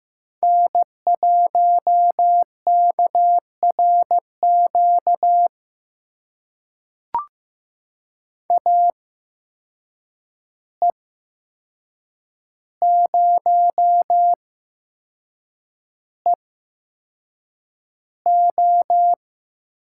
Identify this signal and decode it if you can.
Morse code, heard as a keyed tone, about 15 wpm, spaced as for 6 wpm, 700 Hz. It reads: N1KRQ AE0EO
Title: USA Call Signs - 5 Characters - 8x Character Spacing - 15wpm